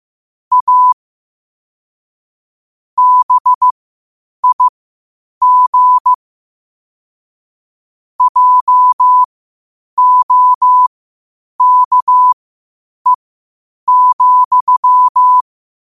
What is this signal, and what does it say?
Morse code, heard as a keyed tone, about 15 wpm, spaced as for 8 wpm, 1000 Hz. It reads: A BIG JOKE,
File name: qotdfiles / 8wpm